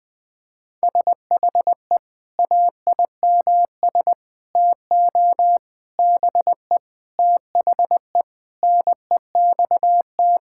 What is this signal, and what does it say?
Morse code, heard as a keyed tone, about 20 wpm, 700 Hz. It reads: SHE AIMS TO BE THE NEXT
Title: Sentences with 9-Word Phrases - 20wpm